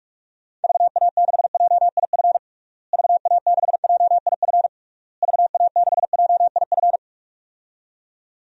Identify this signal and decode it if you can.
Morse code, heard as a keyed tone, about 46 wpm, 700 Hz. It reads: VA6JIF VA6JIF VA6JIF
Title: DXCC-100 Call Signs - 6 Characters - Repeated 3x - 45wpm